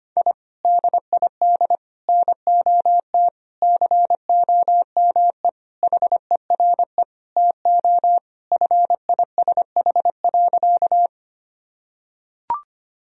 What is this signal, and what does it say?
Morse code, heard as a keyed tone, about 25 wpm, 700 Hz. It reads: I DID NOT COME HERE TO FISH.
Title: Sentences from Top 1000 Words - Review Part 1 - 25wpm